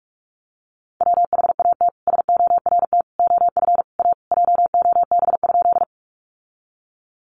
Morse code, heard as a keyed tone, about 46 wpm, 700 Hz. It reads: WHAT SORT OF A JOB?
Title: Sentences from Top 1300 Words 45wpm